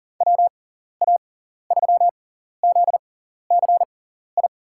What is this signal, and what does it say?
Morse code, heard as a keyed tone, about 40 wpm, spaced as for 14 wpm, 700 Hz. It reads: WA3ZCI
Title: USA Call Signs - 6 Characters - 6x Character Spacing - 40wpm